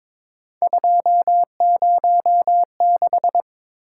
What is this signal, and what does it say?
Morse code, heard as a keyed tone, about 23 wpm, 700 Hz. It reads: 206
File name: practice / 22wpm / Three Letter-Number Combinations - 22wpm